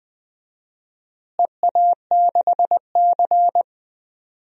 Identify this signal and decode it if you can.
Morse code, heard as a keyed tone, about 20 wpm, 700 Hz. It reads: EA6C
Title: DXCC-100 Call Signs - 4 Characters - Repeated 2x - 20wpm